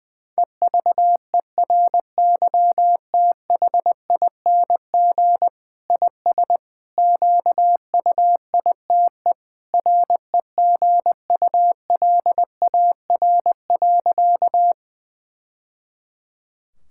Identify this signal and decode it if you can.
Morse code, heard as a keyed tone, about 20 wpm, 700 Hz. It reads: EVERYTHING IS QUITE REGULAR.
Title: Sentences from Top 1000 Words - Review Part 6 - 20wpm